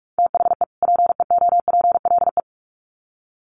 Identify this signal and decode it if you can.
Morse code, heard as a keyed tone, about 44 wpm, 700 Hz. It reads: THE PEOPLE
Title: Sets of 2 Words 45wpm